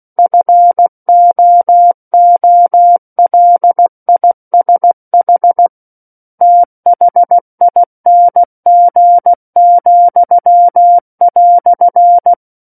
Morse code, heard as a keyed tone, about 16 wpm, 700 Hz. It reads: FOOLISH THING,"
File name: Book One / 07-Chapter 7 - 16 WPM